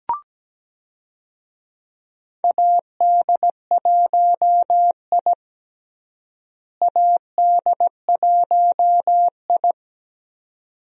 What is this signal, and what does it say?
Morse code, heard as a keyed tone, about 17 wpm, 700 Hz. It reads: AD1I AD1I
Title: USA Call Signs - 4 Characters - Repeated 2x - 17wpm